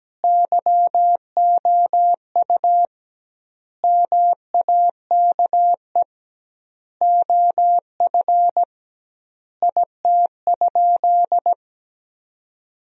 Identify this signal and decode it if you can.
Morse code, heard as a keyed tone, about 17 wpm, 700 Hz. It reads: YOU MAKE OF IT?
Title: 2x Word-Spacing - Sentences from Top 100 Words - 17wpm